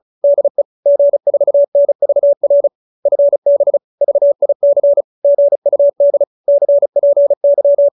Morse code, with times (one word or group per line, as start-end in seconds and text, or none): 0.2-0.6 DE
0.9-2.7 G4NVR
3.1-3.8 FB
4.0-5.0 VIC
5.2-6.2 GUD
6.5-7.9 CPY